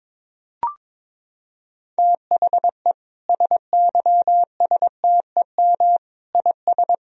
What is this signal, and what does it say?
Morse code, heard as a keyed tone, about 22 wpm, 700 Hz. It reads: THE SYSTEM IS
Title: Sentences from Top 1000 Words - Review Part 3 - 22wpm